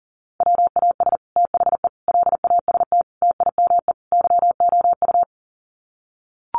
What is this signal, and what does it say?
Morse code, heard as a keyed tone, about 40 wpm, 700 Hz. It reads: WAS THE LAST TIME YOU
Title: Sets of 5 Words Encore2 40wpm